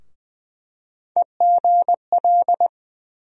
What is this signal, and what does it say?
Morse code, heard as a keyed tone, about 20 wpm, 700 Hz. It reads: EGL
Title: USA Call Sign Postfixes - 3 Letters 20wpm